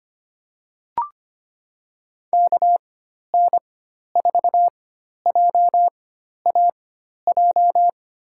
Morse code, heard as a keyed tone, about 25 wpm, spaced as for 12 wpm, 700 Hz. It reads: KN4JAJ
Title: USA Call Signs - 6 Characters - 4x Character Spacing - 25wpm